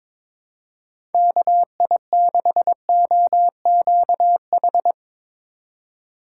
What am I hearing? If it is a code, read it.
Morse code, heard as a keyed tone, about 22 wpm, 700 Hz. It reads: KI6OQH
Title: USA Call Signs - 6 Characters - 3x Character Spacing - 22wpm